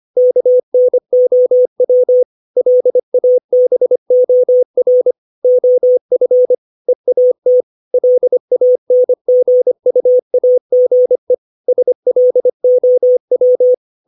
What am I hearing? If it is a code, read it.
Morse code, heard as a keyed tone, about 25 wpm, 500 Hz. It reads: KNOW LABOR OF EAT LANGUAGE SLOW
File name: 632 words / 632 Words 25WPM